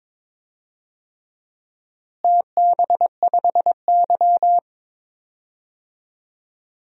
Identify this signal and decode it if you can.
Morse code, heard as a keyed tone, about 22 wpm, 700 Hz. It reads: TB5Y
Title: DXCC-100 Call Signs - 4 Characters - Repeated 3x - 22wpm